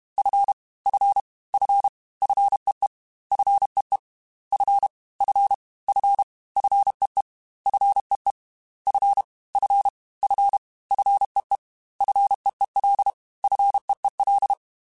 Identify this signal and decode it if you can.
Morse code, heard as a keyed tone, about 32 wpm, 800 Hz. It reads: F F F FEE FEE F F F FEE FEE F F F FEE FEEL FEEL
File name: F and L / F and L mp3 / F and L Prefixes 3x2 Twice @30